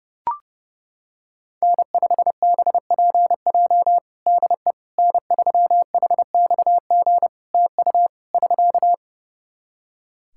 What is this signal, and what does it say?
Morse code, heard as a keyed tone, about 31 wpm, 700 Hz. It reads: N5BPJ DE N3HXG TU <SK>
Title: K1USN SST QSO Practice 30wpm